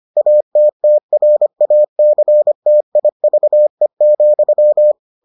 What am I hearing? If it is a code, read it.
Morse code, heard as a keyed tone, about 25 wpm, 600 Hz. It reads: ATTRACTIVE,